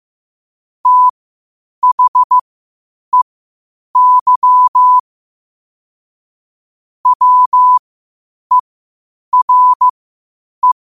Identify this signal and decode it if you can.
Morse code, heard as a keyed tone, about 15 wpm, spaced as for 8 wpm, 1000 Hz. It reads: THEY WERE
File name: qotdfiles / 8wpm